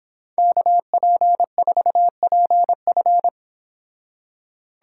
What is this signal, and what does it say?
Morse code, heard as a keyed tone, about 26 wpm, 700 Hz. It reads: KP4PF